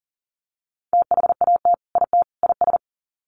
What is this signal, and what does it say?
Morse code, heard as a keyed tone, about 40 wpm, 700 Hz. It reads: THAT IT IS
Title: Sets of 3 Words 40wpm